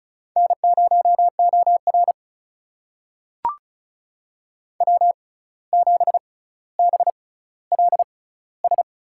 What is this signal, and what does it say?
Morse code, heard as a keyed tone, about 36 wpm, spaced as for 17 wpm, 700 Hz. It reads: N0OR W7BLS
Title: USA Call Signs - 6x Character Spacing - 35wpm